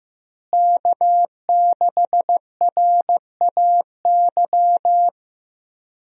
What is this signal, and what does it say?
Morse code, heard as a keyed tone, about 15 wpm, 700 Hz. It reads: K6RAY
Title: USA Call Signs - 5 Characters - Repeated 2x - 15wpm